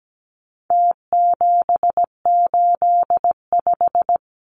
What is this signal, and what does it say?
Morse code, heard as a keyed tone, about 17 wpm, 700 Hz. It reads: T785
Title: Numbers - Contest Style - 4-Digit 17wpm